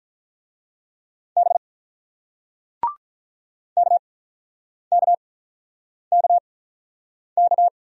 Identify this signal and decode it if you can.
Morse code, heard as a keyed tone, about 46 wpm, spaced as for 9 wpm, 700 Hz. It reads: B KKKK